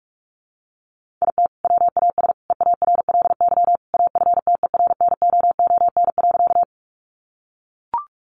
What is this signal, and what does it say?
Morse code, heard as a keyed tone, about 44 wpm, 700 Hz. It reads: IT WAS EARLY AFTERNOON.
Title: Sentences from Top 1500 Words 45wpm